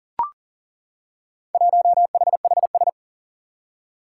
Morse code, heard as a keyed tone, about 38 wpm, 700 Hz. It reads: 1HHS
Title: Four Challenging Letter-Number Combinations 40wpm